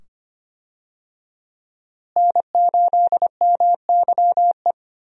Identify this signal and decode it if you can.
Morse code, heard as a keyed tone, about 25 wpm, 700 Hz. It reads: N8MYE
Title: USA Call Signs - 5 Characters - Repeated 2x - 25wpm